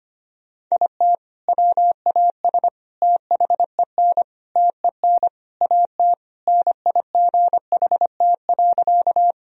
Morse code, heard as a keyed tone, about 25 wpm, 700 Hz. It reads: IT WAS THEN TEN AT NIGHT.
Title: Sentences from Top 1000 Words - Review Part 2 - 25wpm